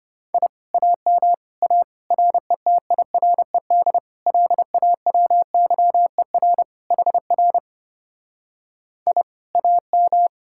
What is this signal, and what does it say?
Morse code, heard as a keyed tone, about 33 wpm, 700 Hz. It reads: I AM A RETIRED LAWYER HR I AM